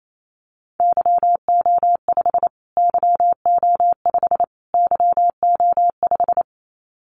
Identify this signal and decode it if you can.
Morse code, heard as a keyed tone, about 28 wpm, 700 Hz. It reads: YO5 YO5 YO5
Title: Callsign Prefixes for Top 25 DXCC - Repeated 3x - 28wpm